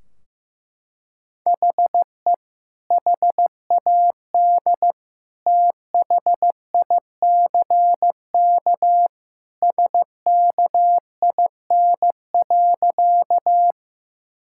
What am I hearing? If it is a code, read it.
Morse code, heard as a keyed tone, about 16 wpm, 700 Hz. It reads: HE HAD THICK SKIN.
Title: Sentences from Top 1600 Words 15wpm